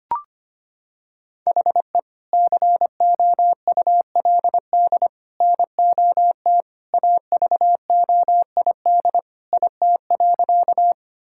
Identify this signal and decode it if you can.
Morse code, heard as a keyed tone, about 25 wpm, 700 Hz. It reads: HE COULD NOT AVOID IT.
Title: Sentences from Top 1000 Words - Review Part 6 - 25wpm